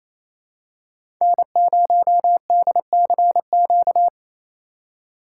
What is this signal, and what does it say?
Morse code, heard as a keyed tone, about 28 wpm, 690 Hz. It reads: N0DCQ
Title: USA Call Signs - 5 Characters - 4x Character Spacing - 28wpm